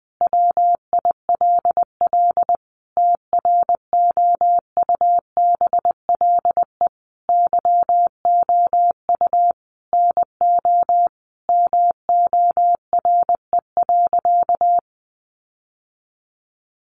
Morse code, heard as a keyed tone, about 20 wpm, 700 Hz. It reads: WILL TROUBLE YOU NO MORE.
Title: Sentences from Top 1200 Words 20wpm